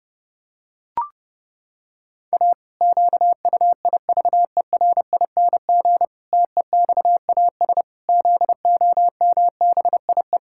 Morse code, heard as a keyed tone, about 32 wpm, 700 Hz. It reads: A QUIVERING TEXAS ZOMBIE
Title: Pangram Challenge v2 - 30wpm